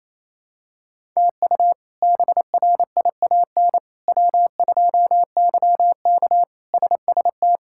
Morse code, heard as a keyed tone, about 28 wpm, 700 Hz. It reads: TU BRIAN W2YK SST
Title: K1USN SST QSO Practice - 28wpm